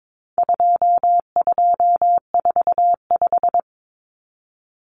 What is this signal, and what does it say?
Morse code, heard as a keyed tone, about 22 wpm, 700 Hz. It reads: 2245
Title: Numbers - 4-Digit 22wpm